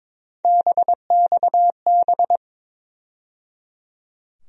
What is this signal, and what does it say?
Morse code, heard as a keyed tone, about 22 wpm, 700 Hz. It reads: BXB
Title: USA Call Sign Postfixes - 3 Letters 22wpm